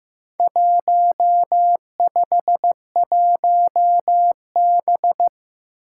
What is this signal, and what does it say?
Morse code, heard as a keyed tone, about 15 wpm, 700 Hz. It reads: 151B
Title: Four Challenging Letter-Number Combinations 15wpm